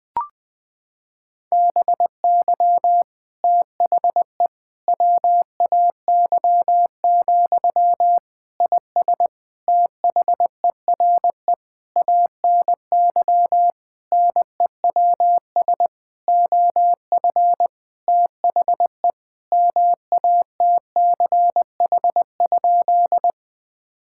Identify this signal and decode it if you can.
Morse code, heard as a keyed tone, about 20 wpm, 700 Hz. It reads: BY THE WAY, IS THERE ANY NEWS OF THE MATCH?
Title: Sentences from Top 1600 Words 20wpm